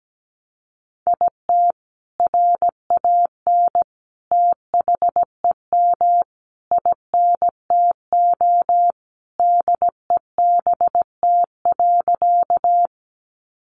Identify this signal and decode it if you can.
Morse code, heard as a keyed tone, about 17 wpm, 700 Hz. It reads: IT RAN THEM INTO DEBT.
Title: Sentences from Top 1300 Words 17wpm